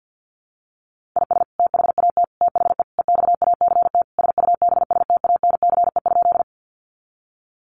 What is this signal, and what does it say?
Morse code, heard as a keyed tone, about 48 wpm, 700 Hz. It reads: IS THAT THE EXACT SUBSTANCE?
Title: Sentences from Top 1500 Words 50wpm